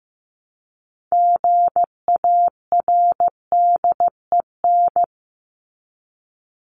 Morse code, heard as a keyed tone, about 15 wpm, 700 Hz. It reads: GARDEN
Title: Top 501-750 Words 15wpm